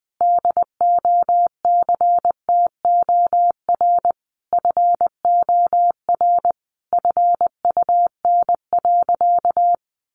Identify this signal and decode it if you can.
Morse code, heard as a keyed tone, about 20 wpm, 700 Hz. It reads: DOCTOR FOR FUN.
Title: Additional Easy Phrases 20wpm